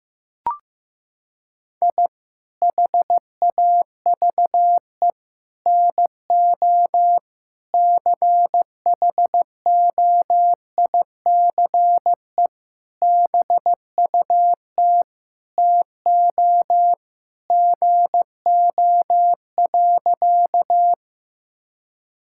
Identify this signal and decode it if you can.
Morse code, heard as a keyed tone, about 15 wpm, 700 Hz. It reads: I HAVE NO CHOICE BUT TO GO.
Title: Sentences from Top 1000 Words - Review Part 6 - 15wpm